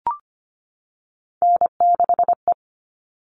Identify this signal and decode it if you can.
Morse code, heard as a keyed tone, about 25 wpm, 700 Hz. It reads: N6E
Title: All Cut Numbers - 3-Digit 25wpm